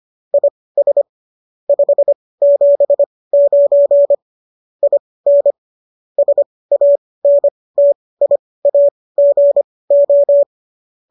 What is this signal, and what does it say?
Morse code, heard as a keyed tone, about 25 wpm, spaced as for 18 wpm, 570 Hz. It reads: IS 579 IN SANTIAGO